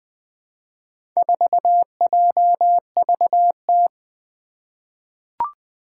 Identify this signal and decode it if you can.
Morse code, heard as a keyed tone, about 20 wpm, 700 Hz. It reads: 4JVT
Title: Four Challenging Letter-Number Combinations 20wpm